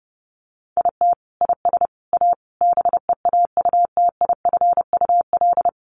Morse code, heard as a keyed tone, about 30 wpm, 700 Hz. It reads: IT IS A BEAUTIFUL
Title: Sentences from Top 1200 Words 30wpm